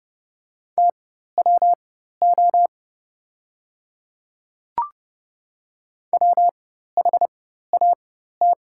Morse code, heard as a keyed tone, about 30 wpm, spaced as for 14 wpm, 700 Hz. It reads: TWO WHAT
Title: 4x Character Spacing - Top 100 Words - 30wpm